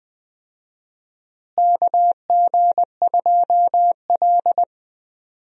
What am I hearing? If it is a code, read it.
Morse code, heard as a keyed tone, about 20 wpm, 700 Hz. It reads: KG2L
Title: USA Call Signs - 4 Characters - Repeated 2x - 20wpm